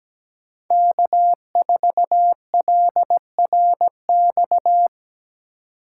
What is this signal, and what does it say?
Morse code, heard as a keyed tone, about 17 wpm, 700 Hz. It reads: K4LRX
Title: K1USN SST QSO Practice - 17wpm